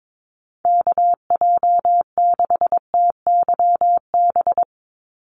Morse code, heard as a keyed tone, about 22 wpm, 700 Hz. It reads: KJ6TYB
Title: USA Call Signs 22wpm